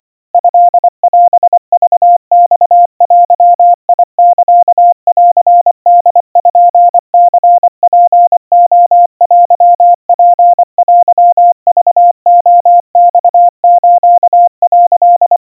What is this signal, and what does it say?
Morse code (Japanese Wabun code), heard as a keyed tone, about 24 wpm, 700 Hz. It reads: トオクマデサンポニツレテツテクレマス」
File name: 90 cpm 700 Hz Crispy